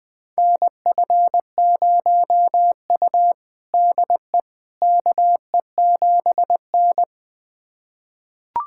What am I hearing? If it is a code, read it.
Morse code, heard as a keyed tone, about 20 wpm, 700 Hz. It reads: NF0U DE KE7N
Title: USA Callsign DE Callsign - 4 Characters - 20wpm